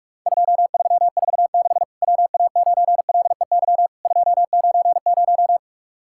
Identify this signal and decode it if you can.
Morse code, heard as a keyed tone, about 46 wpm, 700 Hz. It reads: 2346 WA9LEY 290
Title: Worked All Europe DX Contest - QTC Exchange - 45wpm